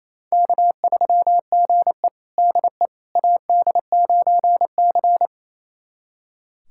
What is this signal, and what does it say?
Morse code, heard as a keyed tone, about 28 wpm, 700 Hz. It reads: K3GE DE AD9C
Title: USA Callsign DE Callsign - 4 Characters - 28wpm